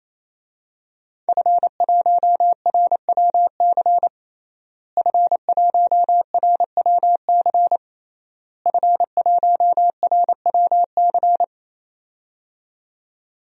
Morse code, heard as a keyed tone, about 28 wpm, 700 Hz. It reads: F1RWC F1RWC F1RWC
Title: DXCC-100 Call Signs - 5 Characters - Repeated 3x - 28wpm